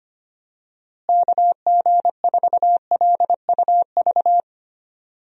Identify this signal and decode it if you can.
Morse code, heard as a keyed tone, about 25 wpm, 700 Hz. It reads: KG4LUV
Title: USA Call Signs - 6 Characters - Repeated 3x - 25wpm